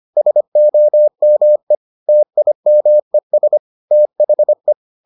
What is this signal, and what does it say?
Morse code, heard as a keyed tone, about 25 wpm, 600 Hz. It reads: SOME TIMES THE